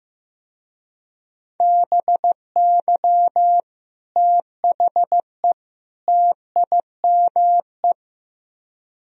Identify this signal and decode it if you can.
Morse code, heard as a keyed tone, about 17 wpm, 700 Hz. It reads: BY THE TIME
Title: Speed-Racing - Sets of 3 Words 15wpm